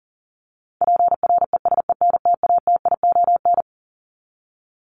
Morse code, heard as a keyed tone, about 40 wpm, 700 Hz. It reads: PRESENTATION
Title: Top 1501-1750 Words 40wpm